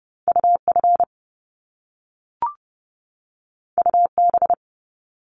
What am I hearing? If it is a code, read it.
Morse code, heard as a keyed tone, about 30 wpm, 700 Hz. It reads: UF UB
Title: Top Two Letter Combinations in Words 30wpm